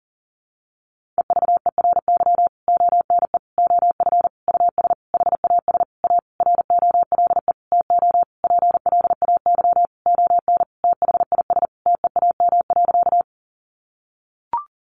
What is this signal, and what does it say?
Morse code, heard as a keyed tone, about 40 wpm, 700 Hz. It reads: EVERY ONE OF US HAS A ROLE TO PLAY ON THIS TEAM.
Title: Sentences from Top 1500 Words 40wpm